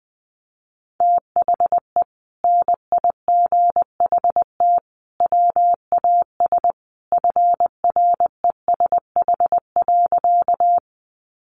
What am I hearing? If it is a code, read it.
Morse code, heard as a keyed tone, about 20 wpm, 700 Hz. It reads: THE NIGHT WAS FRESH.
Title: Sentences from Top 1200 Words 20wpm